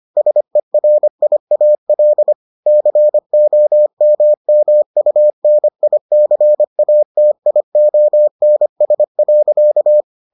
Morse code, heard as a keyed tone, about 25 wpm, 600 Hz. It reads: SERIAL COMMUNICATIONS.